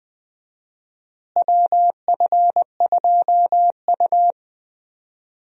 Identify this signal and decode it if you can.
Morse code, heard as a keyed tone, about 20 wpm, 690 Hz. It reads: WF2U
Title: USA Call Signs - 4 Characters - Repeated 3x - 20wpm